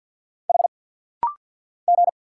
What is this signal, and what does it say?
Morse code, heard as a keyed tone, about 50 wpm, 700 Hz. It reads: H G
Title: Single Letter-Number Alt - Rapid-Fire 50wpm